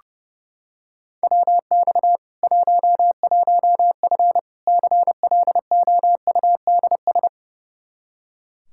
Morse code, heard as a keyed tone, about 30 wpm, 700 Hz. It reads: WX 11F CLOUDS
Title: Practice QSO - Exchange 2 - Info - 30wpm